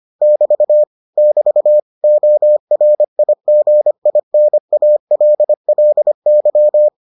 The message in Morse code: <BT> <BT> ORIGINALLY

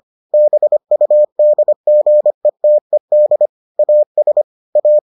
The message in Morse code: BUDGETED AS A

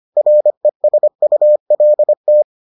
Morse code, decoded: RESULT

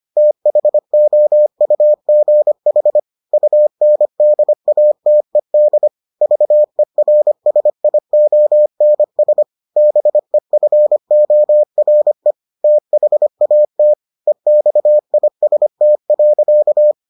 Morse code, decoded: THOUGH UNDATED VERSIONS BEFORE THAT EXIST.

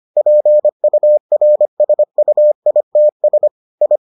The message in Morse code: PURSUITS I

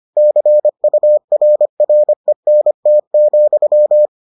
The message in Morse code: CURRENT,

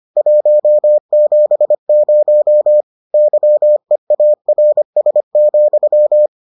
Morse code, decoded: 170 YEARS,

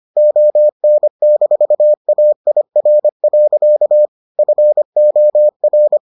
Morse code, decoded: ON-AIR. FOR

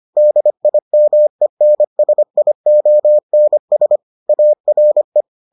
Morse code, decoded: DIMENSIONS ARE

